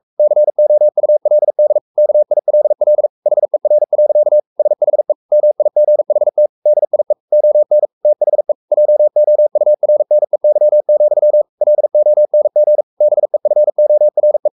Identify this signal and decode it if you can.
Morse code, heard as a keyed tone, about 44 wpm, 600 Hz. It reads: COULD KILL HER. SHE MIGHT DIE ON THE JOURNEY, LONG BEFORE
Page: Romanen Never af Ken Follet på engelsk.